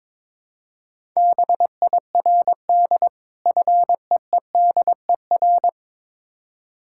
Morse code, heard as a keyed tone, about 22 wpm, 690 Hz. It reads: BIRD FEEDER
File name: Difficult Words with Hints - 22wpm